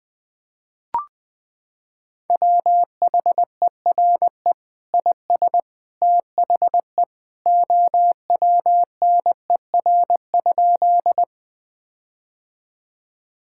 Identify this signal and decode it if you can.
Morse code, heard as a keyed tone, about 19 wpm, 700 Hz. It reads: WHERE IS THE OWNER?
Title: Sentences from Top 1600 Words 20wpm